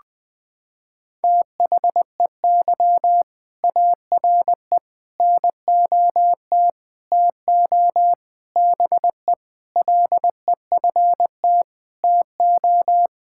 Morse code, heard as a keyed tone, about 20 wpm, 700 Hz. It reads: THEY ARE NOT TO BE LEFT TO
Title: Sentences from Top 1000 Words - Review Part 5 - 20wpm